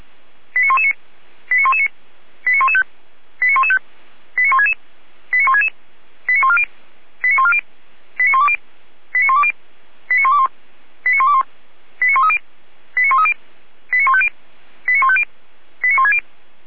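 Five-tone selective call (ZVEI), heard as quick runs of five tones, tones of about 70 ms. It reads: 89119 89119 89115 89115 89166 89166 89140 89140 89120 89120 89121 89121 89130 89130 89150 89150 89160
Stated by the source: ZVEI five-tone selective call received on SDR.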